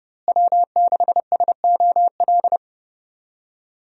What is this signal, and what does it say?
Morse code, heard as a keyed tone, about 30 wpm, 700 Hz. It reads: W6SOL
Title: USA Call Signs - 5 Characters - Repeated 3x - 30wpm